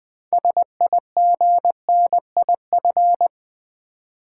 Morse code, heard as a keyed tone, about 20 wpm, 700 Hz. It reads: SIGNIF
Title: Build A Word - 6 Letters At A Time - 12-Letter Words - 20wpm